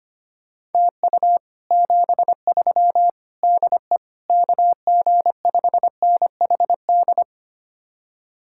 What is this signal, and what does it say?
Morse code, heard as a keyed tone, about 32 wpm, 700 Hz. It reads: TU 73 DE KG5NHD